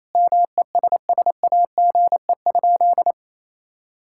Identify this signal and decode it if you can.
Morse code, heard as a keyed tone, about 28 wpm, 700 Hz. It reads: MESSAGE?
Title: Sentences from Top 1600 Words 28wpm